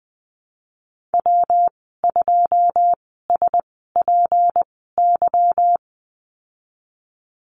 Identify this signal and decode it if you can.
Morse code, heard as a keyed tone, about 20 wpm, spaced as for 15 wpm, 700 Hz. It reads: W2SPY
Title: 179 Lesson 29g - Slash - Call Signs 20wpm